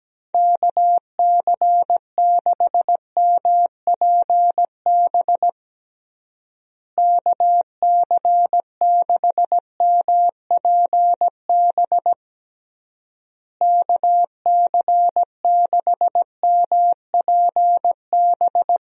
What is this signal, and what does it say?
Morse code, heard as a keyed tone, about 17 wpm, 700 Hz. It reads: KC6MPB KC6MPB KC6MPB
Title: USA Call Signs - 6 Characters - Repeated 3x - 17wpm